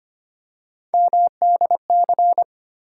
Morse code, heard as a keyed tone, about 25 wpm, 700 Hz. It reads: MDC
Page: morse-code-ninja-practice-25wpm